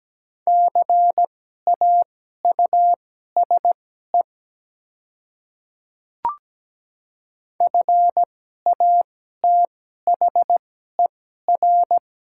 Morse code, heard as a keyed tone, about 17 wpm, spaced as for 12 wpm, 700 Hz. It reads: CAUSE FATHER
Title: 2x Character Spacing - Top 200 Words - 17wpm